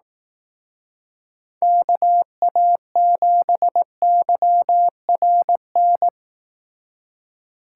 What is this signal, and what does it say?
Morse code, heard as a keyed tone, about 19 wpm, 700 Hz. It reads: KA7YRN